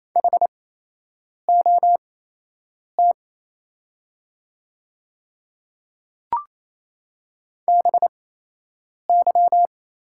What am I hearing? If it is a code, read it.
Morse code, heard as a keyed tone, about 27 wpm, spaced as for 8 wpm, 700 Hz. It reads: HOT BY
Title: 8x Character Spacing - Top 200 Words - 28wpm